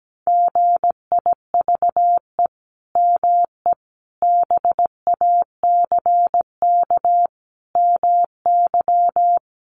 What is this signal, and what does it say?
Morse code, heard as a keyed tone, about 17 wpm, 700 Hz. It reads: GIVE ME BACK MY
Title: Sentences from Top 800 Words 17wpm